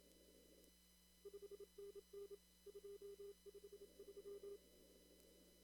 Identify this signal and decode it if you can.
Morse code, heard as a keyed tone, about 27 wpm, 400 Hz. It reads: 5NN 253
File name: CW / kt5j wpx cw-op-k5pi